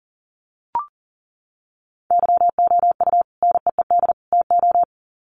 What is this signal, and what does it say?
Morse code, heard as a keyed tone, about 40 wpm, 700 Hz. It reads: YOU NEED TO
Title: Sets of 3 Words 40wpm